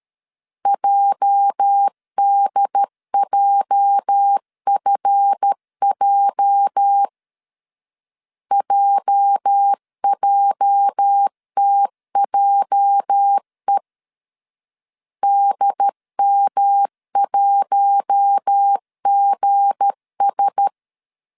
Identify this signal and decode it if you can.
Morse code, heard as a keyed tone, about 13 wpm, 800 Hz. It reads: JDJFJ JJTJE DM1GS
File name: LEZIONI / 8 YJ45 / 8 YJ 12wpm 2235